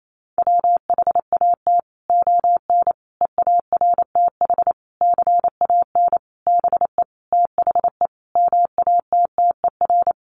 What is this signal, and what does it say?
Morse code, heard as a keyed tone, about 28 wpm, 700 Hz. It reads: WHAT ON EARTH CAN BE THE MATTER
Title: Sentences from Top 700 Words 28wpm